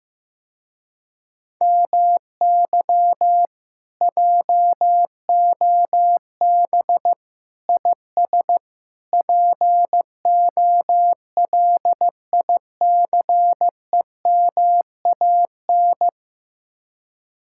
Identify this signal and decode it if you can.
Morse code, heard as a keyed tone, about 15 wpm, 700 Hz. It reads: MY JOB IS POLICEMAN
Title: Speed-Racing Common QSO Phrases Encore - 15wpm